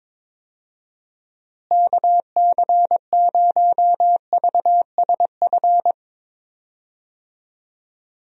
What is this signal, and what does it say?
Morse code, heard as a keyed tone, about 21 wpm, 700 Hz. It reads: KC0VSF
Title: USA Call Signs - 6 Characters - Repeated 2x - 22wpm